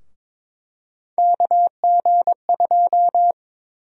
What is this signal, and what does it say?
Morse code, heard as a keyed tone, about 22 wpm, 700 Hz. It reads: KG2
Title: USA Call Sign Prefixes - 22wpm